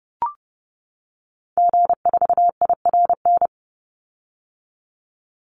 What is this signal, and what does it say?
Morse code, heard as a keyed tone, about 30 wpm, 700 Hz. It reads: G4IRN
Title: Call Signs - 30wpm